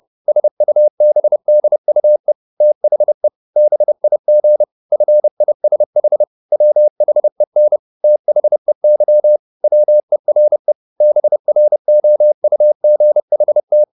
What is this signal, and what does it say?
Morse code, heard as a keyed tone, about 30 wpm, 600 Hz. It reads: SUBDUE THE BIG FISH WHEN THEY WERE BROUGHT
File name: the old man and the sea / the old man and the sea 1 30wpm